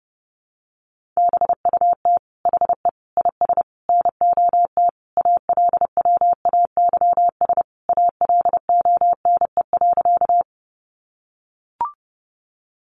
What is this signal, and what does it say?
Morse code, heard as a keyed tone, about 30 wpm, 700 Hz. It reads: BUT HE IS NOT ALWAYS ALONE.
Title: Sentences from Top 1000 Words 30wpm